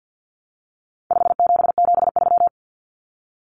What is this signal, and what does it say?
Morse code, heard as a keyed tone, about 50 wpm, 700 Hz. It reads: 5773
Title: Common Cut Numbers - 4-Digit 50wpm